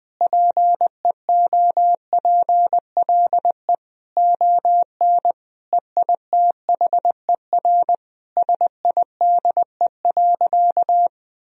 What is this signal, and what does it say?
Morse code, heard as a keyed tone, about 20 wpm, 700 Hz. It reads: PEOPLE ON EITHER SIDE.
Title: Sentences from Top 1000 Words - Review Part 4 - 20wpm